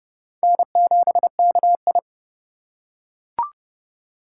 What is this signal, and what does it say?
Morse code, heard as a keyed tone, about 30 wpm, 700 Hz. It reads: N7KI
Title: USA Call Signs - 4 Characters - 2x Character Spacing - 30wpm